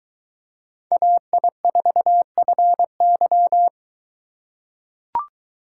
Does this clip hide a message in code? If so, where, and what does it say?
Yes: Morse code from 0.9 to 3.7 s, heard as a keyed tone, about 22 wpm, 700 Hz. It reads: AI4FY